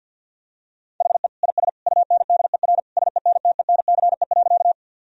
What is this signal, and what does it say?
Morse code, heard as a keyed tone, about 51 wpm, 700 Hz. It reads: HE IS UNDER SENTENCE.
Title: Sentences from Top 1000 Words - Review Part 1 - 50wpm